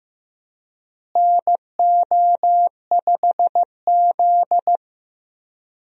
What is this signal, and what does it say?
Morse code, heard as a keyed tone, about 16 wpm, 700 Hz. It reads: NO5Z
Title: USA Call Signs - 4 Characters - Repeated 3x - 15wpm